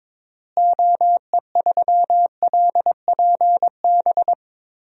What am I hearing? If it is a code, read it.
Morse code, heard as a keyed tone, about 21 wpm, 700 Hz. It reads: OE3LPB
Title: DXCC-100 Call Signs - 6 Characters - Repeated 2x - 22wpm